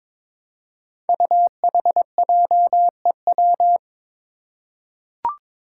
Morse code, heard as a keyed tone, about 22 wpm, 700 Hz. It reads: UHJEW
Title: Five Challenging Letter-Number Combinations 22wpm